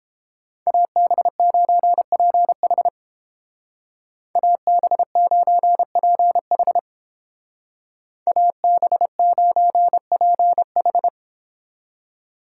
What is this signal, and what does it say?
Morse code, heard as a keyed tone, about 30 wpm, 700 Hz. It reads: AB9PH AB9PH AB9PH